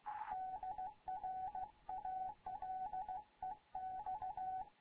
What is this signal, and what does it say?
Morse code, heard as a keyed tone, about 16 wpm, 750 Hz, with radio noise. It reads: DR ALEX
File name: Morze-HA2MN / hangp / 2 11 GYAKORLAT